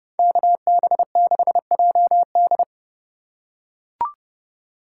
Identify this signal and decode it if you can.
Morse code, heard as a keyed tone, about 30 wpm, 700 Hz. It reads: KB6JD